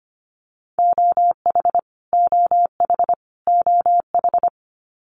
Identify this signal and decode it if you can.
Morse code, heard as a keyed tone, about 25 wpm, 700 Hz. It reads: OH OH OH
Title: Top 500 Words Repeated 25wpm